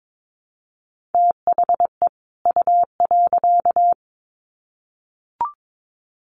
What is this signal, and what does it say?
Morse code, heard as a keyed tone, about 22 wpm, 700 Hz. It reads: THE U.
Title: Sets of 2 Words 22wpm